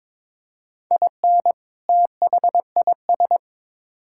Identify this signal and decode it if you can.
Morse code, heard as a keyed tone, about 22 wpm, 700 Hz. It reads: IN THIS
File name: Build A Phrase - 2-Word Phrases - 22wpm